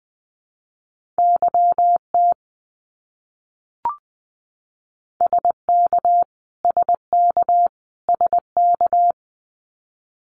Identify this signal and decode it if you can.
Morse code, heard as a keyed tone, about 20 wpm, 700 Hz. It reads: YT SK SK SK
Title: Canadian Provinces and Territories - Repeated 3x - 20wpm